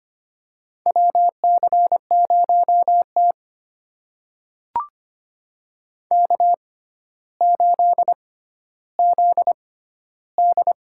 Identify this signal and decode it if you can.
Morse code, heard as a keyed tone, about 25 wpm, spaced as for 13 wpm, 700 Hz. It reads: WC0T K8ZD
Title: USA Call Signs - 4 Characters - 6x Character Spacing - 25wpm